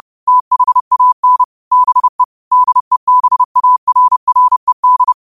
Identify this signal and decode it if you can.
Morse code, heard as a keyed tone, about 30 wpm, 1000 Hz. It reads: THAN BE DEBARRED